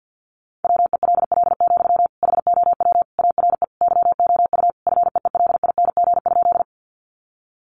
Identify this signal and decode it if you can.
Morse code, heard as a keyed tone, about 50 wpm, 700 Hz. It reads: WELL, HOW ARE YOU FEELING?
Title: Sentences from Top 1200 Words 50wpm